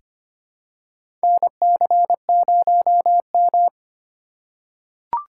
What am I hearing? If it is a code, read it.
Morse code, heard as a keyed tone, about 25 wpm, 700 Hz. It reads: NC0M
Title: USA Call Signs - 4 Characters - 2x Character Spacing - 25wpm